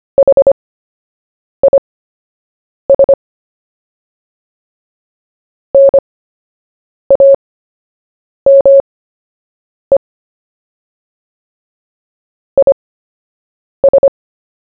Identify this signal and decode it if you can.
Morse code, heard as a keyed tone, about 25 wpm, spaced as for 7 wpm, 550 Hz. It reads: HIS NAME IS
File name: cw / fists / Short-Story-1-7wpm